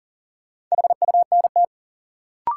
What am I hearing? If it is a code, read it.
Morse code, heard as a keyed tone, about 38 wpm, 700 Hz. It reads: HUNT
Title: Sentences from Top 1600 Words - New Word Drill - 40wpm